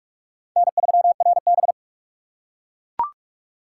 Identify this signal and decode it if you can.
Morse code, heard as a keyed tone, about 44 wpm, 690 Hz. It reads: N3RB
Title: USA Call Signs - 4 Characters - 4x Character Spacing - 45wpm